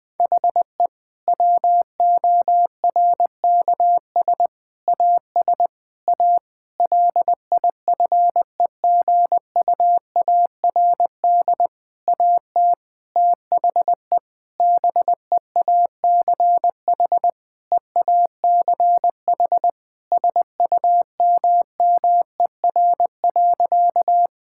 Morse code, heard as a keyed tone, about 21 wpm, 700 Hz. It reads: HE WORKS AS A LIFEGUARD AT THE BEACH EACH SUMMER.
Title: Difficult Sentences with Hints - 20wpm